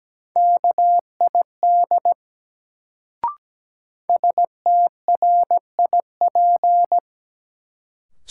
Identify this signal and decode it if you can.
Morse code, heard as a keyed tone, about 17 wpm, 700 Hz. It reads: KID STRIP
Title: Sentences from Top 1600 Words - New Word Drill - 17wpm